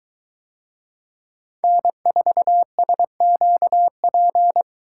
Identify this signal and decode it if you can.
Morse code, heard as a keyed tone, about 21 wpm, 700 Hz. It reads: N4SQP